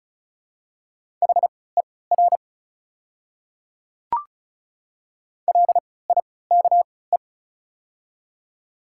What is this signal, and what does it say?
Morse code, heard as a keyed tone, about 36 wpm, spaced as for 20 wpm, 690 Hz. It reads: HER LIKE
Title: 3x Character Spacing - Top 100 Words - 35wpm